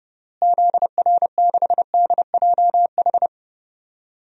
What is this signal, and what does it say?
Morse code, heard as a keyed tone, about 30 wpm, 700 Hz. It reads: ZR6DJH
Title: DXCC-100 Call Signs - 6 Characters - Repeated 2x - 30wpm